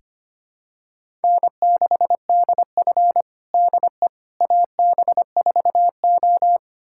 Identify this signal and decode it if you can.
Morse code, heard as a keyed tone, about 25 wpm, 700 Hz. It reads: N6DF DE AB4O
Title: USA Callsign DE Callsign - 4 Characters - 25wpm